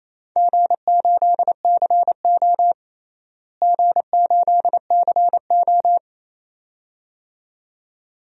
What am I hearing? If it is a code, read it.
Morse code, heard as a keyed tone, about 28 wpm, 700 Hz. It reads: G8CO G8CO
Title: DXCC-100 Call Signs - 4 Characters - Repeated 2x - 28wpm